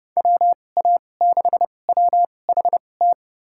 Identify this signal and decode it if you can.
Morse code, heard as a keyed tone, about 30 wpm, spaced as for 22 wpm, 700 Hz. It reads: WA6WHT
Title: USA Call Signs - 6 Characters - 2x Character Spacing - 30wpm